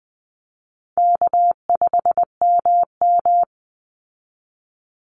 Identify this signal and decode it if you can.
Morse code, heard as a keyed tone, about 20 wpm, 700 Hz. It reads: K5MM
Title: Easy Call Signs Repeated 3x - 20wpm